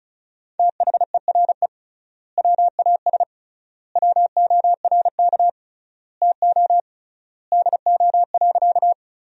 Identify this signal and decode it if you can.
Morse code, heard as a keyed tone, about 36 wpm, 700 Hz. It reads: THERE WAS WORK TO DO.
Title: 3x Word-Spacing - Sentences from Top 200 Words - 35wpm